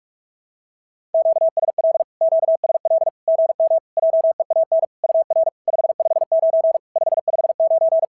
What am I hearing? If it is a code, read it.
Morse code, heard as a keyed tone, about 44 wpm, 650 Hz. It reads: QSL QSL GM JEAN UR 559 559
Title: SOTA QSO Practice 45wpm